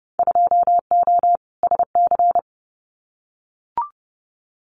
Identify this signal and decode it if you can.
Morse code, heard as a keyed tone, about 32 wpm, 700 Hz. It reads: 2O SC
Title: Winter Field Day Exchanges 30wpm